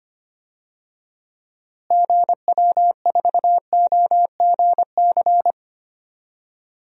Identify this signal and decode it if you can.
Morse code, heard as a keyed tone, about 25 wpm, 700 Hz. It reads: GW4OGC
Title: DXCC-100 Call Signs - 6 Characters - Repeated 3x - 25wpm